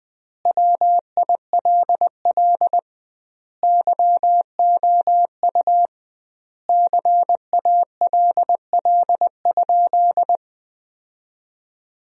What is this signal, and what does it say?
Morse code, heard as a keyed tone, about 21 wpm, 700 Hz. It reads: WILL YOU CALL?
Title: 2x Word-Spacing - Sentences from Top 100 Words - 20wpm